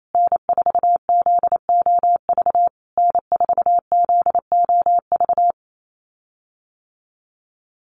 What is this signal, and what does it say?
Morse code, heard as a keyed tone, about 28 wpm, 700 Hz. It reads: N4ZOV N4ZOV
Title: USA Call Signs Repeated 3x 28wpm